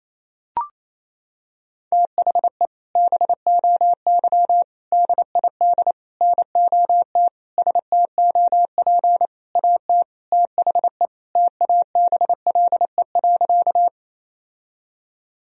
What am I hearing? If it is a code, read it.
Morse code, heard as a keyed tone, about 28 wpm, 700 Hz. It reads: THE BOY DID NOT STOP AT THE TABLE.
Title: Sentences from Top 1000 Words - Review Part 2 - 28wpm